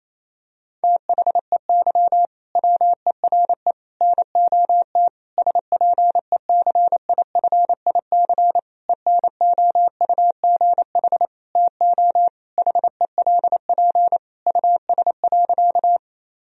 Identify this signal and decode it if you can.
Morse code, heard as a keyed tone, about 27 wpm, 700 Hz. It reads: THEY WERE NOT SPECIFIC ENOUGH TO HELP US.
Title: Sentences from Top 1000 Words - Review Part 3 - 28wpm